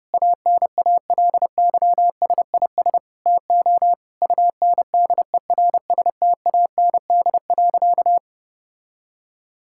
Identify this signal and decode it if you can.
Morse code, heard as a keyed tone, about 30 wpm, 700 Hz. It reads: ANALYSIS TO UNDERSTAND.
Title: Sentences from Top 1000 Words - Review Part 5 - 30wpm